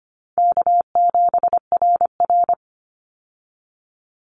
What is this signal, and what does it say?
Morse code, heard as a keyed tone, about 25 wpm, 700 Hz. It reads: K7RR
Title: Easy Call Signs Repeated 2x - 25wpm